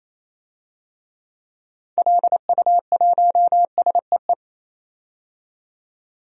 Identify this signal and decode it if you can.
Morse code, heard as a keyed tone, about 28 wpm, 700 Hz. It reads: LU1SEE
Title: DXCC-100 Call Signs - 6 Characters - 28wpm